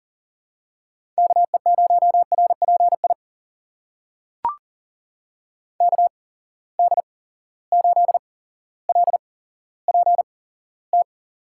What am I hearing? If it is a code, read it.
Morse code, heard as a keyed tone, about 39 wpm, spaced as for 17 wpm, 700 Hz. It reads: KE0RPI KD8LPT
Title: USA Call Signs - 6 Characters - 8x Character Spacing - 40wpm